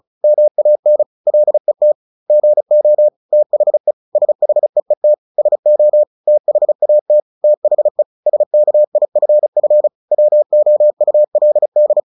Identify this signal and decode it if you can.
Morse code, heard as a keyed tone, about 34 wpm, 600 Hz. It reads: MAN LET GO THE SHEET SO THAT THE SKIFF WOULD
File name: the old man and the sea 2 35wpm